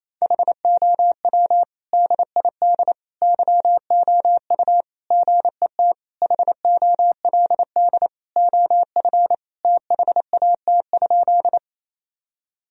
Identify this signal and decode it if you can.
Morse code, heard as a keyed tone, about 28 wpm, 700 Hz. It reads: HOW DID YOU GET HOLD OF THAT?
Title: Sentences from Top 1000 Words - Review Part 2 - 28wpm